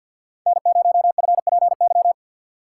Morse code, heard as a keyed tone, about 53 wpm, 700 Hz. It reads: N0FPY
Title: USA Call Signs - 5 Characters - 3x Character Spacing - 50wpm